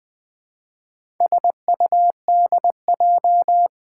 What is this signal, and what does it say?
Morse code, heard as a keyed tone, about 20 wpm, 700 Hz. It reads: SUDJ
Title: Four Letter-Number Combinations - 20wpm